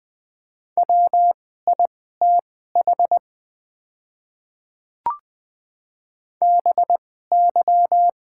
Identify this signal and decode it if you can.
Morse code, heard as a keyed tone, about 21 wpm, spaced as for 15 wpm, 700 Hz. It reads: WITH BY
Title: 2x Character Spacing - Top 100 Words - 20wpm